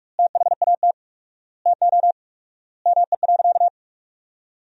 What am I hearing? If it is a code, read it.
Morse code, heard as a keyed tone, about 46 wpm, 700 Hz. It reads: THAT TO ME.
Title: 4x Word-Spacing - Sentences from Top 400 Words - 45wpm